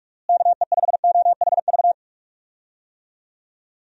Morse code, heard as a keyed tone, about 46 wpm, 700 Hz. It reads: KE5OHV
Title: USA Call Signs - 6 Characters - Repeated 3x - 45wpm